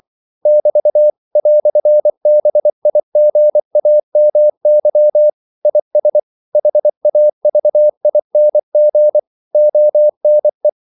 Morse code, heard as a keyed tone, about 24 wpm, 600 Hz. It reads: <BT> "BIGAMY IS HAVING ONE
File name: CW / Practice / EnglishQuotations0324